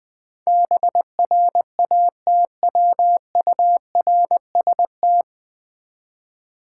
Morse code, heard as a keyed tone, about 20 wpm, 700 Hz. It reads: BRATWURST
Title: Difficult Words with Hints - 20wpm